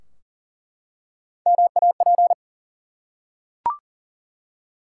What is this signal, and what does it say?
Morse code, heard as a keyed tone, about 38 wpm, 700 Hz. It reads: MAP
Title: Three Letter Words - 40wpm